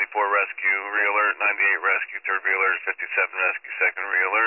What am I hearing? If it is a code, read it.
Emergency Alert System (SAME) tones, heard as harsh buzzing bursts, with speech. It is heard from 0.5 to 4.2 s.